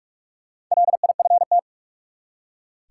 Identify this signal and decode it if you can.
Morse code, heard as a keyed tone, about 46 wpm, 700 Hz. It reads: LIFT
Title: Four Letter Words - 45wpm